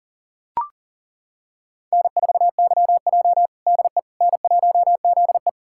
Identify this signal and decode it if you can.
Morse code, heard as a keyed tone, about 40 wpm, 700 Hz. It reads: N4YJ DE N1ZE
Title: USA Callsign DE Callsign - 4 Characters - 40wpm